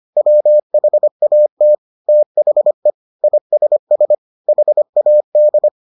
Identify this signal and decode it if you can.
Morse code, heard as a keyed tone, about 25 wpm, 600 Hz. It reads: WHAT THE ISS HAD